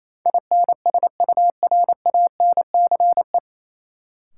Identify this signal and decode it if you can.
Morse code, heard as a keyed tone, about 28 wpm, 700 Hz. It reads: INSURANCE
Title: Sentences from Top 1100 Words - New Word Drill - 28wpm